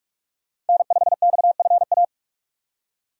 Morse code, heard as a keyed tone, about 57 wpm, 700 Hz. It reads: N5XFA